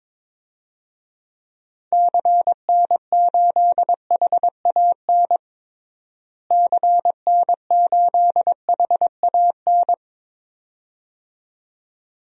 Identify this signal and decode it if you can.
Morse code, heard as a keyed tone, about 22 wpm, 700 Hz. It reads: CN8HAN CN8HAN
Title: DXCC-100 Call Signs - 6 Characters - Repeated 2x - 22wpm